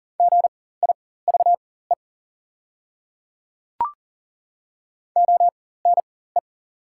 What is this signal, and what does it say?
Morse code, heard as a keyed tone, about 40 wpm, spaced as for 19 wpm, 690 Hz. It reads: GIVE ONE
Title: 4x Character Spacing - Top 500 Words - 40wpm